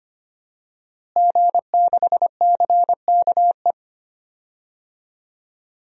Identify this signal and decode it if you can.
Morse code, heard as a keyed tone, about 25 wpm, 700 Hz. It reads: G6CKE
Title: DXCC-100 Call Signs - 5 Characters - Repeated 2x - 25wpm